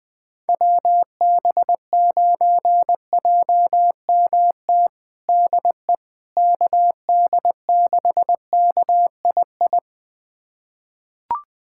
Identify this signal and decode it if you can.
Morse code, heard as a keyed tone, about 20 wpm, 700 Hz. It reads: WB9JMT DE KD6KII
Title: USA Callsign DE Callsign - 6 Characters - 20wpm